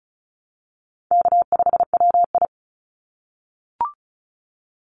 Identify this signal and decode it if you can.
Morse code, heard as a keyed tone, about 34 wpm, 700 Hz. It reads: K5WI
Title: NA SOTA Chaser Callsigns 35wpm